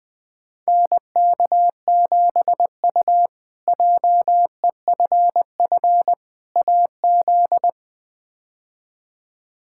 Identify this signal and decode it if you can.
Morse code, heard as a keyed tone, about 20 wpm, 700 Hz. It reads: NK7U JEFF AZ
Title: K1USN SST QSO Practice - 20wpm